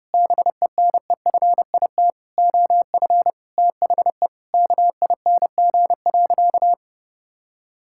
Morse code, heard as a keyed tone, about 30 wpm, 700 Hz. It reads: BENEFIT OF THE KING.
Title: Sentences from Top 1000 Words - Review Part 6 - 30wpm